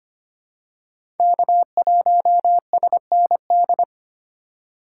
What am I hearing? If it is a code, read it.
Morse code, heard as a keyed tone, about 25 wpm, 700 Hz. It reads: K1SND
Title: K1USN SST QSO Practice - 25wpm